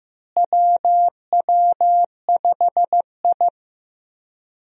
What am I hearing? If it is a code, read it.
Morse code, heard as a keyed tone, about 15 wpm, 700 Hz. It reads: WW5I
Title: USA Call Signs - 4 Characters - Repeated 3x - 15wpm